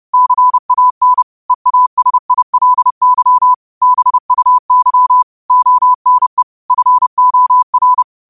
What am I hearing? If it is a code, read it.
Morse code, heard as a keyed tone, about 30 wpm, 1000 Hz. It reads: CAN EASILY BUY ONE FOR
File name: qotdfiles / 30wpm